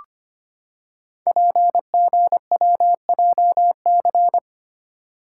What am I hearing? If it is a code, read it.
Morse code, heard as a keyed tone, about 25 wpm, 700 Hz. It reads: PGWJC
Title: Code Groups - 5 Letters - 25wpm